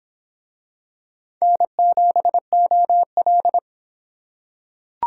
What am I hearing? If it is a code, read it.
Morse code, heard as a keyed tone, about 26 wpm, 700 Hz. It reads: N7OL